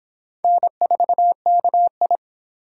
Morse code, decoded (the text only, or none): N4KI